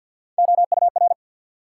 Morse code, heard as a keyed tone, about 53 wpm, 700 Hz. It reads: OUR
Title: Sets of 2 Words - New Word Drill - 50wpm